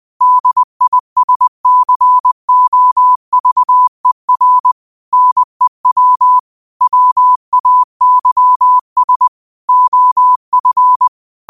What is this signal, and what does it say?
Morse code, heard as a keyed tone, about 20 wpm, 1000 Hz. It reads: DISCOVER NEW WAYS OF
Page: 20 WPM morse code quotes for Tue, 12 Nov 2024.